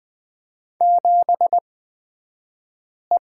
Morse code, heard as a keyed tone, about 20 wpm, spaced as for 5 wpm, 700 Hz. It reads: 7E
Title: 242 Lesson 39d - Letter z - Two Characters 20wpm